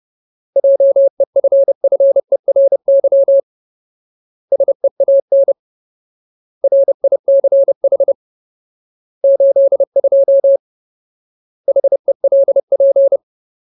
Morse code, heard as a keyed tone, about 30 wpm, 550 Hz. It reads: JEFFERY SEAN RICH 82 HELP
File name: Ultimate Word List / 30WPM Ultimate